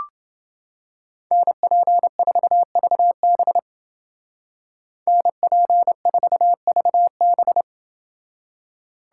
Morse code, heard as a keyed tone, about 29 wpm, 700 Hz. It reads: NP4VB NP4VB